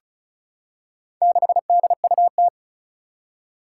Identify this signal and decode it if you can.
Morse code, heard as a keyed tone, about 36 wpm, 700 Hz. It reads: 6DUT
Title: Four Challenging Letter-Number Combinations 35wpm